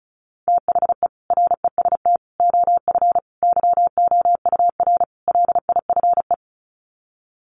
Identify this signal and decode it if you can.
Morse code, heard as a keyed tone, about 34 wpm, 700 Hz. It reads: THE REST OF YOUR LIFE
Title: Sets of 5 Words 35wpm